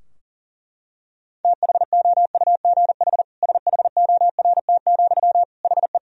Morse code, heard as a keyed tone, about 37 wpm, 700 Hz. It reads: THOUGH SHORT, HE
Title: Sentences from Top 1000 Words - Review Part 3 - 40wpm